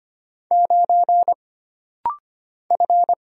Morse code, heard as a keyed tone, about 25 wpm, 700 Hz. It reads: 9 F
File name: Single Character (Letter:Number:Punctuation:Prosign) - Rapid-Fire - 25wpm